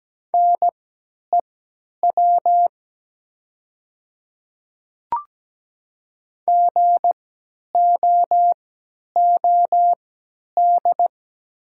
Morse code, heard as a keyed tone, about 17 wpm, spaced as for 10 wpm, 700 Hz. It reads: NEW GOOD
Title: 3x Character Spacing - Top 300 Words - 17wpm